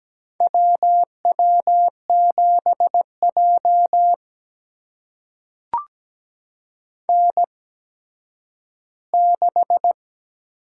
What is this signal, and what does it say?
Morse code, heard as a keyed tone, about 17 wpm, spaced as for 10 wpm, 700 Hz. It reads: WW7J N6
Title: USA Call Signs - 4 Characters - 8x Character Spacing - 17wpm